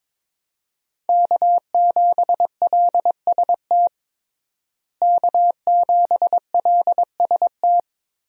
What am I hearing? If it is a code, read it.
Morse code, heard as a keyed tone, about 22 wpm, 700 Hz. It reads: K7LST K7LST
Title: USA Call Signs - 5 Characters - Repeated 2x - 22wpm